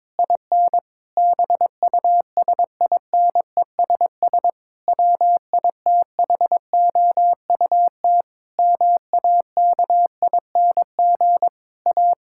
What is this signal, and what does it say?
Morse code, heard as a keyed tone, about 22 wpm, 700 Hz. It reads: IN BUSINESS WITHOUT MAKING A
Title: Sentences from Top 1000 Words - Review Part 6 - 22wpm